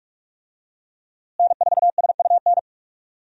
Morse code, heard as a keyed tone, about 42 wpm, 700 Hz. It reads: N4SUN
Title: USA Call Signs - 5 Characters - Repeated 3x - 45wpm